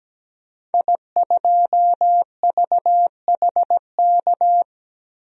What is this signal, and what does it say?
Morse code, heard as a keyed tone, about 17 wpm, 700 Hz. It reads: I2VHK
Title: DXCC-100 Call Signs - 5 Characters - Repeated 3x - 17wpm